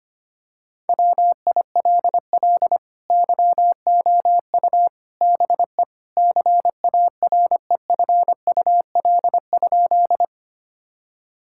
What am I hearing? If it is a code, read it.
Morse code, heard as a keyed tone, about 25 wpm, 700 Hz. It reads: WILL YOU BE CAREFUL?
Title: Sentences from Top 1600 Words 25wpm